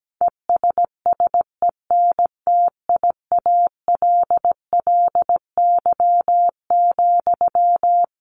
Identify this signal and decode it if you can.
Morse code, heard as a keyed tone, about 17 wpm, 700 Hz. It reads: ESSENTIALLY,
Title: Sentences from Top 1300 Words 17wpm